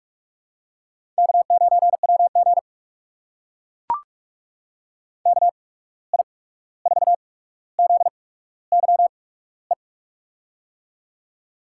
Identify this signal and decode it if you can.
Morse code, heard as a keyed tone, about 44 wpm, spaced as for 17 wpm, 700 Hz. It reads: K9WG KI4ZYE
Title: USA Call Signs - 8x Character Spacing - 45wpm